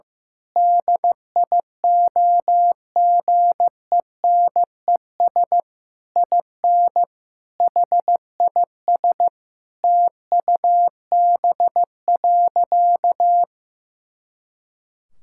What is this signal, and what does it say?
Morse code, heard as a keyed tone, about 15 wpm, 700 Hz. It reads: DIOGENES IN HIS TUB.
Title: Pangram Challenge v2 - 15wpm